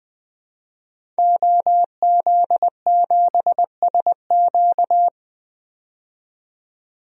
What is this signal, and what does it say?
Morse code, heard as a keyed tone, about 20 wpm, 700 Hz. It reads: OZ7SQ
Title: DXCC-100 Call Signs - 5 Characters - 20wpm